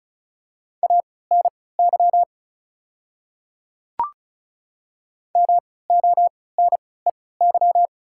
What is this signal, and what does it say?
Morse code, heard as a keyed tone, about 36 wpm, spaced as for 20 wpm, 690 Hz. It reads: ANY MONEY
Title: 3x Character Spacing - Top 500 Words - 35wpm